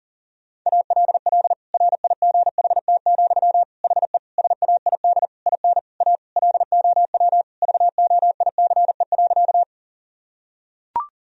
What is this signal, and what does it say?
Morse code, heard as a keyed tone, about 40 wpm, 700 Hz. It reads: ALL RIGHT, HE SAID IN A LOW VOICE.
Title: Sentences from Top 1000 Words - Review Part 1 - 40wpm